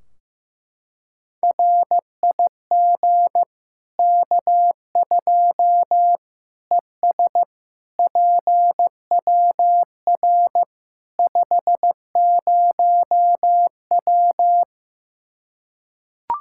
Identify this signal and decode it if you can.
Morse code, heard as a keyed tone, about 15 wpm, 700 Hz. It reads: RIG K2 ES PWR 50W
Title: Practice QSO - Exchange 2 - Info - 15wpm